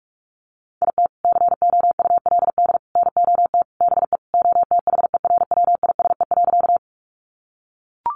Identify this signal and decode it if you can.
Morse code, heard as a keyed tone, about 44 wpm, 700 Hz. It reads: IT COULD NOT BE OTHERWISE.
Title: Sentences from Top 1200 Words 45wpm